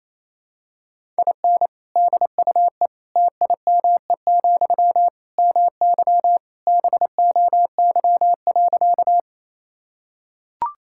Morse code, heard as a keyed tone, about 28 wpm, 700 Hz. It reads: IN DUE TIME, MY BOY.
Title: Sentences from Top 1000 Words - Review Part 6 - 28wpm